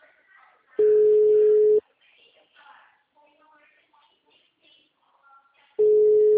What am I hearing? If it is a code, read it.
Ringback tone (Europe, 425 Hz), with speech.
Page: Arapongas - PR